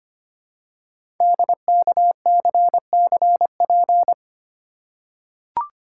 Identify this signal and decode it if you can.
Morse code, heard as a keyed tone, about 25 wpm, 700 Hz. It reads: DKCCP
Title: Code Groups - 5 Letters - 25wpm